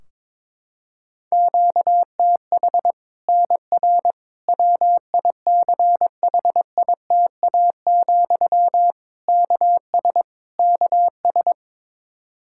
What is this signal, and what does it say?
Morse code, heard as a keyed tone, about 22 wpm, 700 Hz. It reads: QTH NR WICHITA, KS KS
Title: Practice QSO - Exchange 1 - Beginning - 22wpm